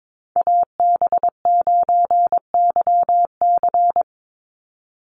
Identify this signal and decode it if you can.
Morse code, heard as a keyed tone, about 22 wpm, 700 Hz. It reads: AB9YC
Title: Call Signs Repeated 2x - 22wpm